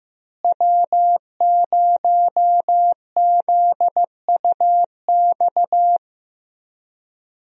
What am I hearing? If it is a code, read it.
Morse code, heard as a keyed tone, about 15 wpm, 700 Hz. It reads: W0ZUX
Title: USA Call Signs - 5 Characters - Repeated 2x - 15wpm